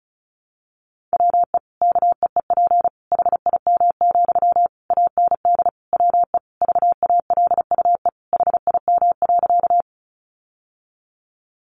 Morse code, heard as a keyed tone, about 34 wpm, 700 Hz. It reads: WE KEEP HIM, AND WE VALUE HIM.
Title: Sentences from Top 600 Words 35wpm